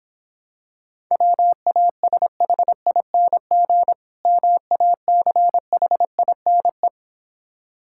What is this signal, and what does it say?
Morse code, heard as a keyed tone, about 26 wpm, 700 Hz. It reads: WASHING MACHINE